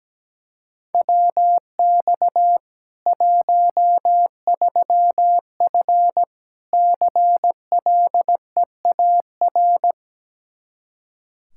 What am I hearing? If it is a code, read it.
Morse code, heard as a keyed tone, about 17 wpm, 700 Hz. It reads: WX 13F CLEAR
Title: K1USN SST QSO Practice 17wpm